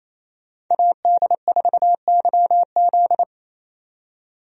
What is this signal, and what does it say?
Morse code, heard as a keyed tone, about 28 wpm, 700 Hz. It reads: AD4YZ
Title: USA Call Signs - 5 Characters - 28wpm